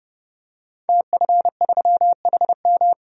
Morse code, heard as a keyed tone, about 30 wpm, 700 Hz. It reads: TF3HM
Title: DXCC-100 Call Signs - 5 Characters - 30wpm